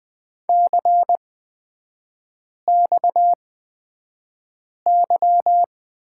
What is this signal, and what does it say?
Morse code, heard as a keyed tone, about 20 wpm, spaced as for 5 wpm, 690 Hz. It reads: CXY
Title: 229 Lesson 37e - Letter x - Three Characters 20wpm